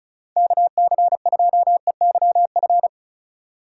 Morse code, heard as a keyed tone, about 36 wpm, 690 Hz. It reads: KC2EYF
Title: USA Call Signs - 6 Characters - 4x Character Spacing - 35wpm